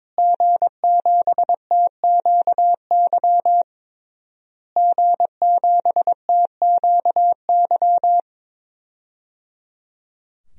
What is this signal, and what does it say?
Morse code, heard as a keyed tone, about 22 wpm, 700 Hz. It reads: G7TQY G7TQY
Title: DXCC-100 Call Signs - 5 Characters - Repeated 3x - 22wpm